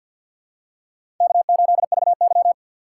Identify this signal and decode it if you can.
Morse code, heard as a keyed tone, about 54 wpm, 700 Hz. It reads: K8VY
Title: USA Call Signs - 6x Character Spacing - 50wpm